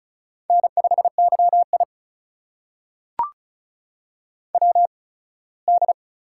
Morse code, heard as a keyed tone, about 36 wpm, spaced as for 21 wpm, 700 Hz. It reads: N5YI WD
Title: USA Call Signs - 4 Characters - 8x Character Spacing - 35wpm